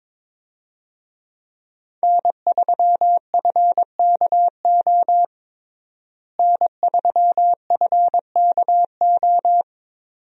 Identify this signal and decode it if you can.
Morse code, heard as a keyed tone, about 22 wpm, 700 Hz. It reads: N3FKO N3FKO
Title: USA Call Signs - 5 Characters - Repeated 2x - 22wpm